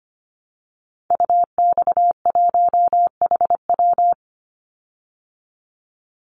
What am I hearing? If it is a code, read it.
Morse code, heard as a keyed tone, about 25 wpm, 700 Hz. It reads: UX1HW
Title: Call Signs Repeated 3x - 25wpm